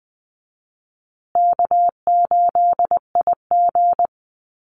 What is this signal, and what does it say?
Morse code, heard as a keyed tone, about 20 wpm, 700 Hz. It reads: K8IG
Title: USA Call Signs Repeated 3x 20wpm